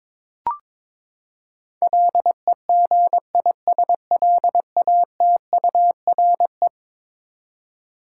Morse code, heard as a keyed tone, about 22 wpm, 700 Hz. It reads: LEGISLATURE
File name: practice / 22wpm / Eleven Letter Words - 22wpm